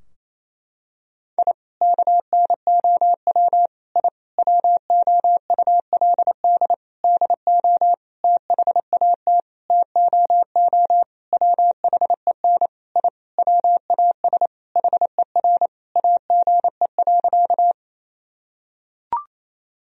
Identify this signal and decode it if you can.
Morse code, heard as a keyed tone, about 27 wpm, 700 Hz. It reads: I KNOW I WOULD DO THAT TOO WHEN I WAS HER AGE.
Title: Sentences from Top 1000 Words - Review Part 3 - 28wpm